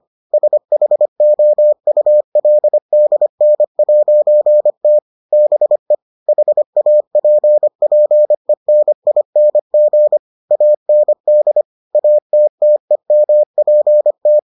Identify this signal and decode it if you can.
Morse code, heard as a keyed tone, about 25 wpm, 600 Hz. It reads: SHOULDN'T BE HAPPENING AND ATTEMPT